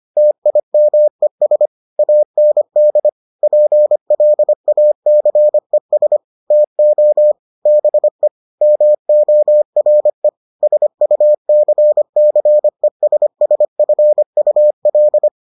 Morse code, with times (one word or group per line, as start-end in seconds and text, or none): 0.2-1.7 TIMES
2.0-3.1 AND
3.4-6.2 PLACES
6.5-7.3 TO
7.7-8.3 BE
8.6-10.3 MORE
10.6-15.3 SUCCESSFUL